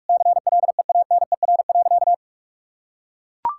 Morse code, heard as a keyed tone, about 44 wpm, 700 Hz. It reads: CLEANER.
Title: Pangram Challenge v2 - 45wpm